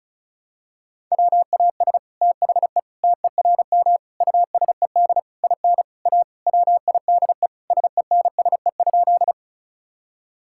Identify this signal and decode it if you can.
Morse code, heard as a keyed tone, about 36 wpm, 700 Hz. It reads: WAS THE TERM USED IN A WIDE SENSE?
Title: Sentences from Top 1000 Words - Review Part 6 - 35wpm